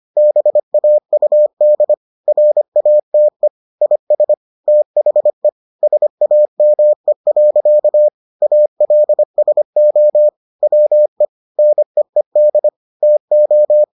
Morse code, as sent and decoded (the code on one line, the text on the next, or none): -... .- ..- -.. / .-. .- - . / .. ... / - .... / ... .- -- . .-.-.- / .- .-.. ... --- / .-- . / -. . -.. / - ---
BAUD RATE IS THE SAME. ALSO WE NEED TO